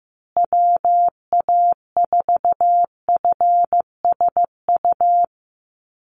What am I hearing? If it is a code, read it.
Morse code, heard as a keyed tone, about 15 wpm, 700 Hz. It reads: WA4FSU
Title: USA Call Signs Repeated 2x 15wpm